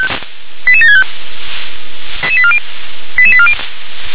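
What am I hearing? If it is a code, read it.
Five-tone selective call (ZVEI), heard as quick runs of five tones, tones of about 70 ms. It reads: 88765 88840
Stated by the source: ZVEI five-tone selective call 88958 received on an SDR.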